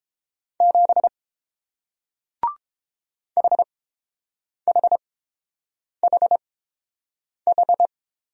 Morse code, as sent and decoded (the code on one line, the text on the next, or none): --... / .... .... .... ....
7 HHHH